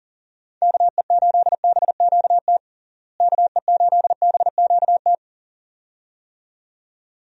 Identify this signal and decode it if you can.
Morse code, heard as a keyed tone, about 40 wpm, 700 Hz. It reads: KE8BQT KE8BQT
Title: USA Call Signs - 6 Characters - Repeated 2x - 40wpm